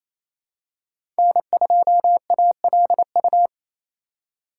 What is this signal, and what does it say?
Morse code, heard as a keyed tone, about 28 wpm, 700 Hz. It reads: N2ALU
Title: USA Call Signs - 5 Characters - 2x Character Spacing - 28wpm